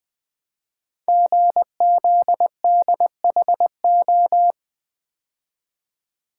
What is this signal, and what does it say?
Morse code, heard as a keyed tone, about 20 wpm, 700 Hz. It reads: GZDHO
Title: Code Groups - 5 Letters - 20wpm